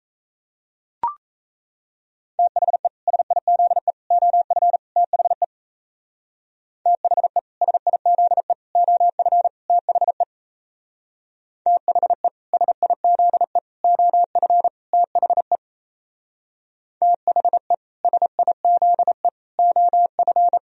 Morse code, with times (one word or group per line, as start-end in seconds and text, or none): 2.4-2.9 THE
3.1-3.9 SIZE
4.1-4.8 OF
5.0-5.4 THE
6.9-7.4 THE
7.6-8.5 SIZE
8.8-9.5 OF
9.7-10.2 THE
11.7-12.3 THE
12.5-13.6 SIZE
13.8-14.7 OF
14.9-15.6 THE
17.0-17.7 THE
18.0-19.3 SIZE
19.6-20.6 OF